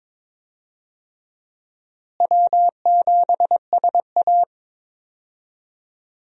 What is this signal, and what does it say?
Morse code, heard as a keyed tone, about 22 wpm, 700 Hz. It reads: W7SA
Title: USA Call Signs - 4 Characters - Repeated 2x - 22wpm